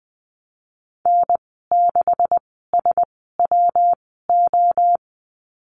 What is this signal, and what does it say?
Morse code, heard as a keyed tone, about 19 wpm, spaced as for 14 wpm, 700 Hz. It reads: N6SWO
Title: 161 Lesson 26g - Letter g - Call Signs 20wpm